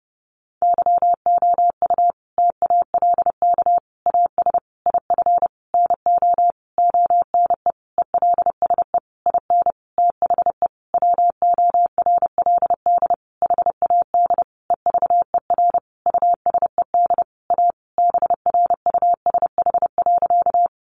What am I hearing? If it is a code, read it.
Morse code, heard as a keyed tone, about 30 wpm, 700 Hz. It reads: YOU TALK AS IF NO ONE ELSE IN THE WORLD HAD EVER USED A BRUSH.
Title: Sentences from Top 1100 Words 30wpm